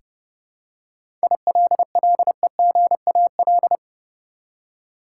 Morse code, heard as a keyed tone, about 30 wpm, 700 Hz. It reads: ILLEGAL
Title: Seven Letter Words - 30wpm